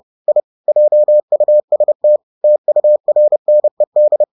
Morse code, heard as a keyed tone, about 30 wpm, 600 Hz. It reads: I JUST TURNED